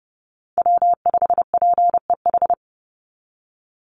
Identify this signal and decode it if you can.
Morse code, heard as a keyed tone, about 30 wpm, 700 Hz. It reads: W5PEH
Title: Call Signs Repeated 3x - 30wpm